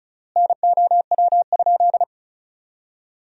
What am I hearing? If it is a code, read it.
Morse code, heard as a keyed tone, about 36 wpm, 700 Hz. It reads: NOW?
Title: 3x Word-Spacing - Sentences from Top 100 Words - 35wpm